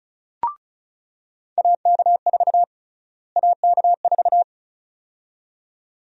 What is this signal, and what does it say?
Morse code, heard as a keyed tone, about 36 wpm, 700 Hz. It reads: AK4 AK4
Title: USA Call Sign Prefixes - Repeated 2x - 35wpm